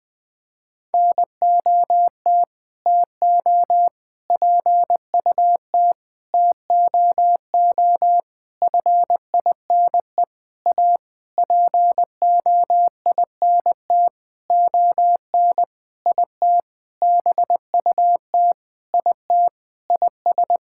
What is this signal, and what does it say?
Morse code, heard as a keyed tone, about 20 wpm, 700 Hz. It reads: NOT TO PUT TOO FINE A POINT ON IT BUT IT IS
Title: Sentences with 9-Word Phrases - Spoken Beginning - 20wpm